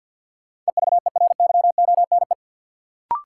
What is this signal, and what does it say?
Morse code, heard as a keyed tone, about 50 wpm, 700 Hz. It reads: EVERYONE
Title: Sentences from Top 800 Words - New Word Drill - 50wpm